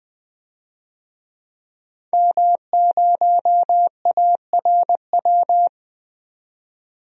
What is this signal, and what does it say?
Morse code, heard as a keyed tone, about 20 wpm, 700 Hz. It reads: M0ARW
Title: DXCC-100 Call Signs - 5 Characters - 20wpm